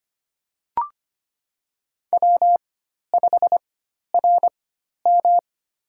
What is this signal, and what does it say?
Morse code, heard as a keyed tone, about 25 wpm, spaced as for 12 wpm, 700 Hz. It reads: W5RM
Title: USA Call Signs - 4 Characters - 4x Character Spacing - 25wpm